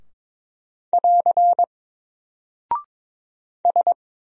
Morse code, heard as a keyed tone, about 22 wpm, 700 Hz. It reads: <AR> S
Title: Single Character (Letter:Number:Punctuation:Prosign) Alt 22wpm